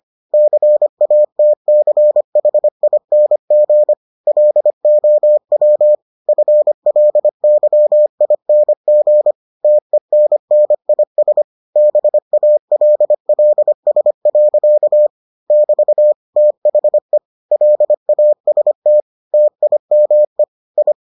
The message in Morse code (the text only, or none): CATCHING LOW FLYING TENNIS BALLS. <BT> THE LAST TIME I